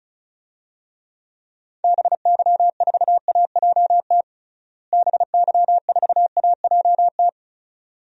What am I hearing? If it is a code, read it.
Morse code, heard as a keyed tone, about 34 wpm, 700 Hz. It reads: BY4AJT BY4AJT
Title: DXCC-100 Call Signs - 6 Characters - Repeated 2x - 35wpm